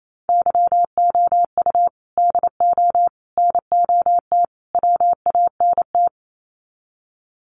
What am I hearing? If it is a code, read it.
Morse code, heard as a keyed tone, about 28 wpm, 700 Hz. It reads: YOU DO NOT WANT
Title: Sets of 4 Words 28wpm